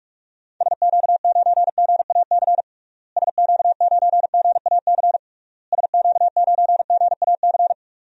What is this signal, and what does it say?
Morse code, heard as a keyed tone, about 46 wpm, 700 Hz. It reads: SQ9GAC SQ9GAC SQ9GAC
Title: DXCC-100 Call Signs - 6 Characters - Repeated 3x - 45wpm